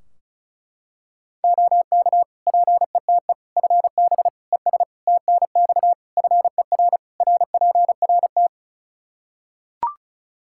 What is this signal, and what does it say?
Morse code, heard as a keyed tone, about 34 wpm, 700 Hz. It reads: OK PETE FB ES TNX FER RPRT
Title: Practice QSO - Exchange 2 - Info - 35wpm